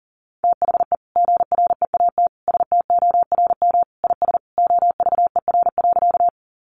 Morse code, heard as a keyed tone, about 40 wpm, 700 Hz. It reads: THE GREAT STORM IS OVER.
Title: Sentences from Top 1500 Words 40wpm